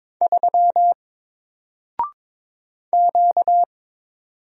Morse code, heard as a keyed tone, about 22 wpm, 700 Hz. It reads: 3 Q
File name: Single-Letter-Number-22wpm